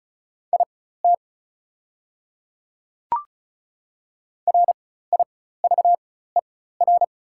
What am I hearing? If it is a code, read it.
Morse code, heard as a keyed tone, about 34 wpm, spaced as for 16 wpm, 690 Hz. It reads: IT RIVER
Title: 4x Character Spacing - Top 300 Words - 35wpm